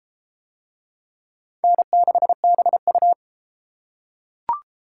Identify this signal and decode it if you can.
Morse code, heard as a keyed tone, about 31 wpm, 700 Hz. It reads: N6BU